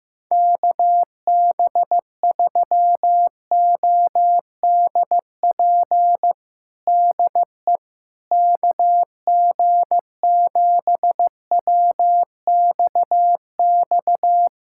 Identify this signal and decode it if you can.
Morse code, heard as a keyed tone, about 15 wpm, 700 Hz. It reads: KB3ODP DE KG7WXX
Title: USA Callsign DE Callsign 15wpm